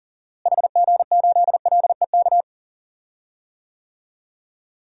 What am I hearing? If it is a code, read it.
Morse code, heard as a keyed tone, about 40 wpm, 700 Hz. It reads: HG8LEK
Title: DXCC-100 Call Signs - 6 Characters - Repeated 2x - 40wpm